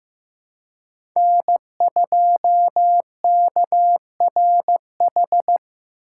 Morse code, heard as a keyed tone, about 15 wpm, 700 Hz. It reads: N2KRH
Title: USA Call Signs - 5 Characters - 2x Character Spacing - 15wpm